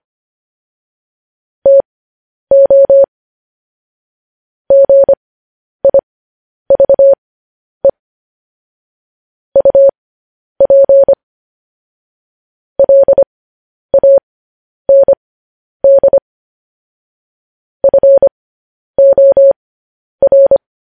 Morse code, heard as a keyed tone, about 25 wpm, spaced as for 10 wpm, 550 Hz. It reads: TO GIVE UP LAND FOR
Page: news headlines in morse code at 10 wpm